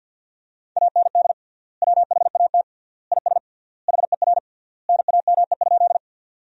Morse code, heard as a keyed tone, about 50 wpm, 700 Hz. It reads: AND WHAT IS HER NAME?
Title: 3x Word-Spacing - Sentences from Top 200 Words - 50wpm